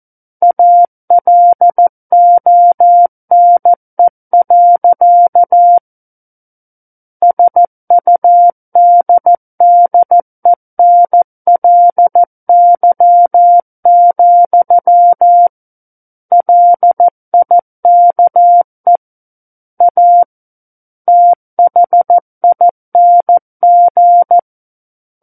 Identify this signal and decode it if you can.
Morse code, heard as a keyed tone, about 14 wpm, 700 Hz. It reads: ALONE. SUDDENLY, LIKE A THING
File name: Book One / 05-Chapter 5 - 14 WPM